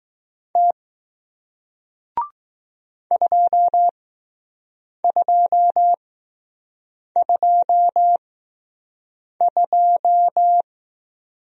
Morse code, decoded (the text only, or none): T 2 2 2 2